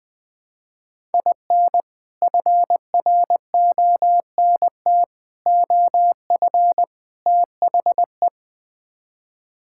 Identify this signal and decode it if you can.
Morse code, heard as a keyed tone, about 20 wpm, 700 Hz. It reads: IN FRONT OF THE
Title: Build A Phrase - 4-Word Phrases - 20wpm